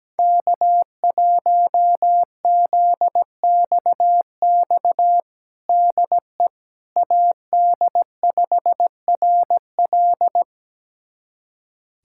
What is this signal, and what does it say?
Morse code, heard as a keyed tone, about 17 wpm, 700 Hz. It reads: K1ZXX DE AD5RL
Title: USA Callsign DE Callsign - 5 Characters - 17wpm